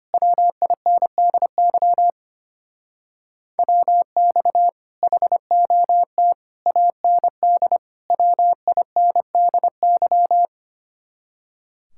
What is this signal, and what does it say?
Morse code, heard as a keyed tone, about 31 wpm, 700 Hz. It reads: WINDY WX HOT AND WINDY
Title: Speed-Racing Common QSO Phrases - 25wpm